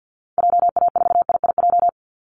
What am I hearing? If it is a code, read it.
Morse code, heard as a keyed tone, about 50 wpm, 700 Hz. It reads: JA4IIJ
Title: Call Signs Repeated 2x - 50wpm